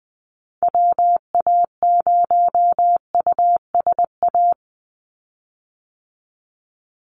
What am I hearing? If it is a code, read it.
Morse code, heard as a keyed tone, about 20 wpm, 700 Hz. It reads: WA0USA
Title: Call Signs Repeated 3x - 20wpm